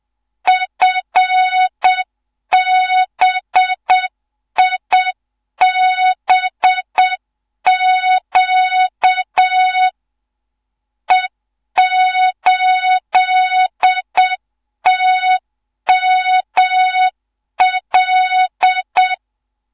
Morse code, heard as a keyed tone, about 7 wpm, 3700 Hz. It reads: FBIBQ E8TML
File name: Morze-HA2MN / hang / 15 GYAKORLAT